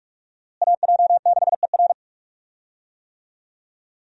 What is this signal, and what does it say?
Morse code, heard as a keyed tone, about 44 wpm, 700 Hz. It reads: AJ6ER
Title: USA Call Signs - 5 Characters - Repeated 3x - 45wpm